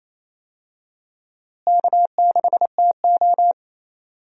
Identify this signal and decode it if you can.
Morse code, heard as a keyed tone, about 28 wpm, 700 Hz. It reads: K6TO
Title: USA Call Signs - 4 Characters - Repeated 2x - 28wpm